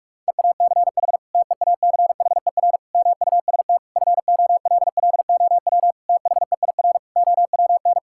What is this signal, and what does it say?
Morse code, heard as a keyed tone, about 41 wpm, 700 Hz. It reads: EACH TEACHER MUST FOLLOW THEIR OWN
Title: Sentences from Top 1000 Words - Review Part 6 - 45wpm